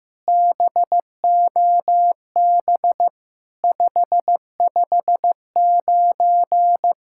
Morse code, heard as a keyed tone, about 14 wpm, 700 Hz. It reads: BOB 559
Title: SOTA QSO Practice 15wpm